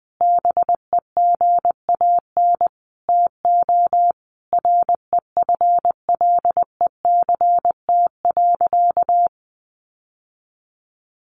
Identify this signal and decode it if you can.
Morse code, heard as a keyed tone, about 20 wpm, 700 Hz. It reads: BEGAN TO REFLECT.
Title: Sentences from Top 1500 Words 20wpm